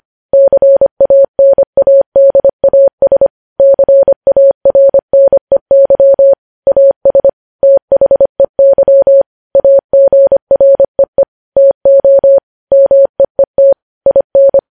Morse code, heard as a keyed tone, about 25 wpm, 550 Hz. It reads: CANADAS CARNEY AS THEY AGREE TO MEET IN
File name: podcast / News headlines-1x-25x25